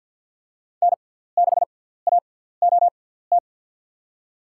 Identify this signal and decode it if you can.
Morse code, heard as a keyed tone, about 52 wpm, spaced as for 17 wpm, 700 Hz. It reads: N6AOT
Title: USA Call Signs - 5 Characters - 6x Character Spacing - 50wpm